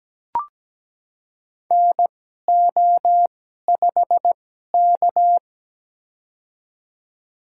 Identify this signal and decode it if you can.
Morse code, heard as a keyed tone, about 17 wpm, spaced as for 12 wpm, 700 Hz. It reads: NO5K
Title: USA Call Signs - 4 Characters - 2x Character Spacing - 17wpm